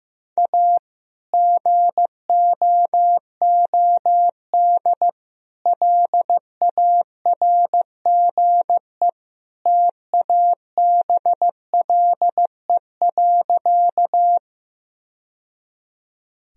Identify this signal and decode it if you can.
Morse code, heard as a keyed tone, about 15 wpm, 700 Hz. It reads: A GOOD LARGE TABLE.
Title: Sentences from Top 1000 Words - Review Part 2 - 15wpm